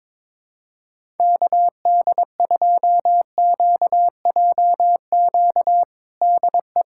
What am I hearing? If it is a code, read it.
Morse code, heard as a keyed tone, about 22 wpm, 700 Hz. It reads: KD2QJQ DE
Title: USA Callsign DE Callsign 22wpm